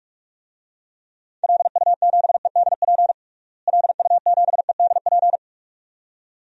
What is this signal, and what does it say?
Morse code, heard as a keyed tone, about 46 wpm, 700 Hz. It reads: LU7EDP LU7EDP
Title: DXCC-100 Call Signs - 6 Characters - Repeated 2x - 45wpm